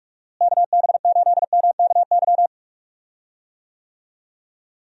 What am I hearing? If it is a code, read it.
Morse code, heard as a keyed tone, about 46 wpm, 700 Hz. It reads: KB8MKY
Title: USA Call Signs - 6 Characters - Repeated 3x - 45wpm